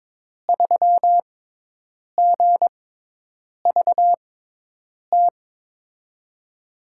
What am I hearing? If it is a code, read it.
Morse code, heard as a keyed tone, about 22 wpm, spaced as for 8 wpm, 700 Hz. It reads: 3GVT
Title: USA Call Signs - 5 Characters - 6x Character Spacing - 22wpm